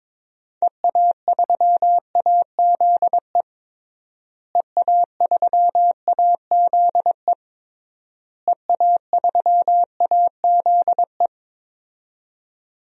Morse code, heard as a keyed tone, about 22 wpm, 700 Hz. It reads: EA3AZE EA3AZE EA3AZE
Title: DXCC-100 Call Signs - 6 Characters - Repeated 3x - 22wpm